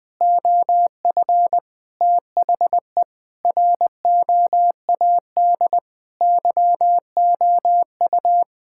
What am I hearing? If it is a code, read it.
Morse code, heard as a keyed tone, about 20 wpm, 700 Hz. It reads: OF THE ROAD YOU
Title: Sentences with 3-Word Phrases - 20wpm